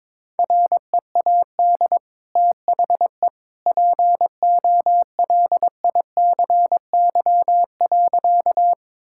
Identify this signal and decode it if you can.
Morse code, heard as a keyed tone, about 21 wpm, 700 Hz. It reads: READ THE POLICY.
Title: Sentences from Top 1000 Words - Review Part 5 - 22wpm